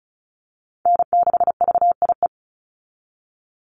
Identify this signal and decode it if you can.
Morse code, heard as a keyed tone, about 34 wpm, 700 Hz. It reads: N6VIE
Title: USA Call Signs 35wpm